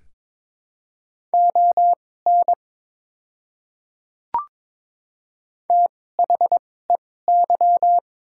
Morse code, heard as a keyed tone, about 22 wpm, spaced as for 16 wpm, 700 Hz. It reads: ON THEY
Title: 2x Character Spacing - Top 200 Words - 22wpm